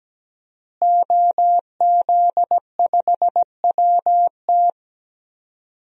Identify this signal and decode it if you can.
Morse code, heard as a keyed tone, about 17 wpm, 700 Hz. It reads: OZ5WT
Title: DXCC-100 Call Signs - 5 Characters - Repeated 3x - 17wpm